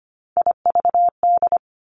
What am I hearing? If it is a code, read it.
Morse code, heard as a keyed tone, about 25 wpm, 700 Hz. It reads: IVD
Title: Three Challenging Letter-Number Combinations 25wpm